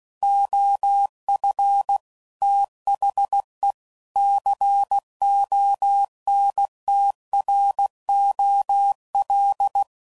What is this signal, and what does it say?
Morse code, heard as a keyed tone, about 16 wpm, 800 Hz. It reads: OF THE CONTROL
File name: CW Code Groups @15WPM / Technical Subjects / OR Gate